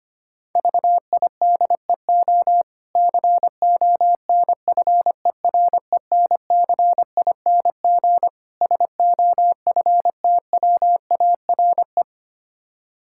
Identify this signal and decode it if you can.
Morse code, heard as a keyed tone, about 25 wpm, 700 Hz. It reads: VIDEO CONFERENCING SOFTWARE
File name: practice / 25wpm / Difficult Words - Speed-Racing with Hints - 25wpm